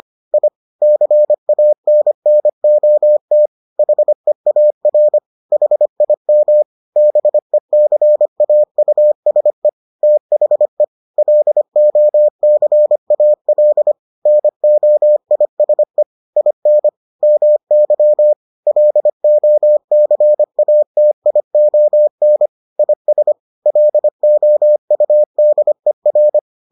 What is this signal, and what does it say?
Morse code, heard as a keyed tone, about 25 wpm, 600 Hz. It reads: I CANNOT HEAR HIM BECAUSE THE LOCAL NOISE IN MY LOCATION IS LOUDER